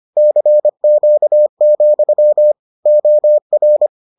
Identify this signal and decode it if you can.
Morse code, heard as a keyed tone, about 25 wpm, 600 Hz. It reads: CQ, OR